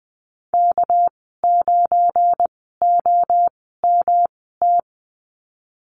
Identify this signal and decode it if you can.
Morse code, heard as a keyed tone, about 20 wpm, spaced as for 14 wpm, 700 Hz. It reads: K9OMT
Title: 238 Lesson 38g - Letter q - Call Signs 20wpm